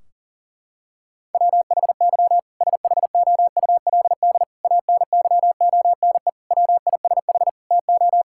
Morse code, heard as a keyed tone, about 40 wpm, 700 Hz. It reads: WHY SHOULD ANYONE WISH TO
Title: Sentences from Top 1600 Words 40wpm